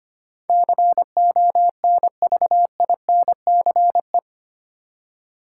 Morse code, heard as a keyed tone, about 25 wpm, 700 Hz. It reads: CONVINCE
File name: Eight Letter Words - 25wpm